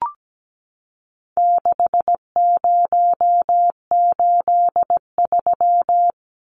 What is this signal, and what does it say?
Morse code, heard as a keyed tone, about 17 wpm, 700 Hz. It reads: 6083
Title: Numbers - 4-Digit 17wpm